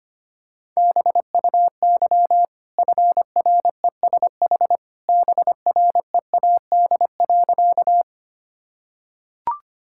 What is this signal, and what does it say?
Morse code, heard as a keyed tone, about 25 wpm, 700 Hz. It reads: BUY FRESH BREAD.
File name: Easy Sentences with Hints - 25wpm